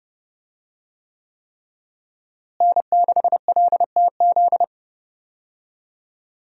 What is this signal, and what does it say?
Morse code, heard as a keyed tone, about 30 wpm, 700 Hz. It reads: N6LTZ
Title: USA Call Signs - 5 Characters - Repeated 3x - 30wpm